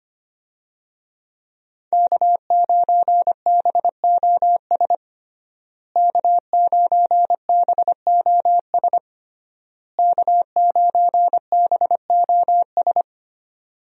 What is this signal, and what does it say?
Morse code, heard as a keyed tone, about 25 wpm, 700 Hz. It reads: K9BOS K9BOS K9BOS
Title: USA Call Signs - 5 Characters - Repeated 3x - 25wpm